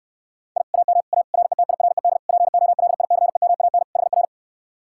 Morse code, heard as a keyed tone, about 69 wpm, 700 Hz. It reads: I AM A RETIRED POLICEMAN HR